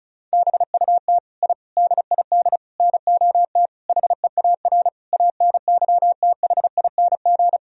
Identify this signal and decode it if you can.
Morse code, heard as a keyed tone, about 36 wpm, 700 Hz. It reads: BUT I DID NOT HEAR ANYTHING
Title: Sentences from Top 1000 Words - Review Part 4 - 35wpm